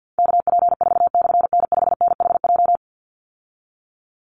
Morse code, heard as a keyed tone, about 49 wpm, 700 Hz. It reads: KP4/N5NHJ
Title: Call Signs Repeated 3x - 50wpm